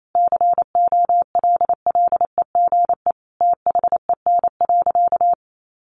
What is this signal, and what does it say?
Morse code, heard as a keyed tone, about 28 wpm, 700 Hz. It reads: COLLEGE THEN.
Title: Sentences from Top 1300 Words 28wpm